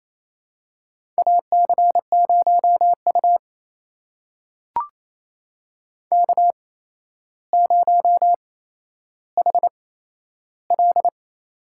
Morse code, heard as a keyed tone, about 28 wpm, spaced as for 12 wpm, 700 Hz. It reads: AC0U K0HL
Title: USA Call Signs - 4 Characters - 8x Character Spacing - 28wpm